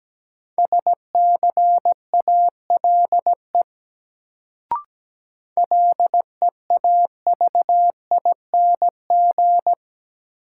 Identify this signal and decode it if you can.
Morse code, heard as a keyed tone, about 17 wpm, 700 Hz. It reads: SCALE LEAVING
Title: Sentences from Top 1000 Words - Review Part 4 - New Word Drill - 17wpm